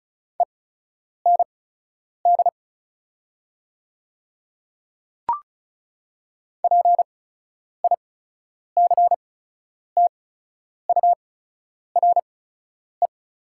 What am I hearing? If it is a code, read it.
Morse code, heard as a keyed tone, about 36 wpm, spaced as for 10 wpm, 700 Hz. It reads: END PICTURE
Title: 8x Character Spacing - Top 200 Words - 35wpm